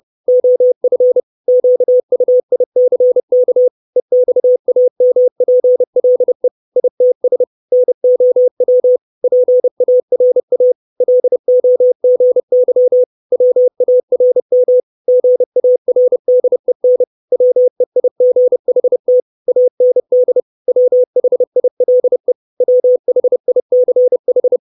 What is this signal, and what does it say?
Morse code, heard as a keyed tone, about 30 wpm, 500 Hz. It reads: OF QUICK EXAMPLE ITS NOW PARA LOGY WARM GARDEN WEIGHT AND WHILE WHICH
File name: 632 Words 30WPM